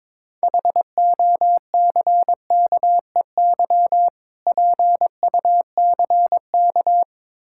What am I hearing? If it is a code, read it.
Morse code, heard as a keyed tone, about 21 wpm, 700 Hz. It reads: HOCKEY PUCK
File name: Difficult Words with Hints - 22wpm